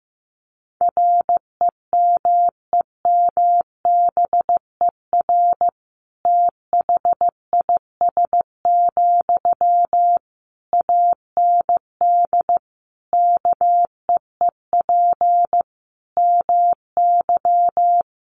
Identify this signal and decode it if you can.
Morse code, heard as a keyed tone, about 15 wpm, 700 Hz. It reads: REMEMBER THIS, AND KEEP MY
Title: Sentences from Top 1500 Words 15wpm